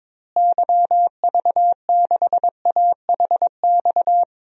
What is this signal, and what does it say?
Morse code, heard as a keyed tone, about 22 wpm, 700 Hz. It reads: YV6AHX
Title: DXCC-100 Call Signs - 6 Characters - Repeated 3x - 22wpm